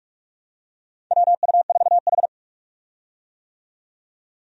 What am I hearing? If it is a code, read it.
Morse code, heard as a keyed tone, about 46 wpm, 700 Hz. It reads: WU4H
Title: USA Call Signs - 4 Characters - 45wpm